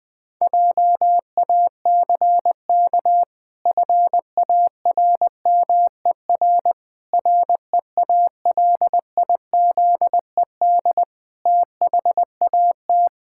Morse code, heard as a keyed tone, about 20 wpm, 700 Hz. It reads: JACK FARMER REALIZED THAT
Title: Pangram Challenge v2 - 20wpm